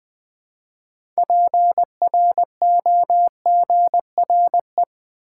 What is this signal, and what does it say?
Morse code, heard as a keyed tone, about 20 wpm, 700 Hz. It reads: PROGRE
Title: Build A Word - 2 Letters At A Time - 8-Letter Words - 20wpm